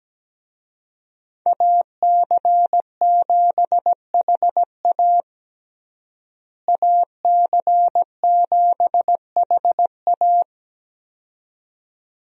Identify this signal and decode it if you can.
Morse code, heard as a keyed tone, about 17 wpm, 700 Hz. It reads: AC7HA AC7HA
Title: USA Call Signs - 5 Characters - Repeated 2x - 17wpm